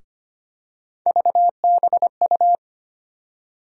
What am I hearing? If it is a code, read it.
Morse code, heard as a keyed tone, about 25 wpm, 700 Hz. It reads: VBU
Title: USA Call Sign Postfixes - 3 Letters 25wpm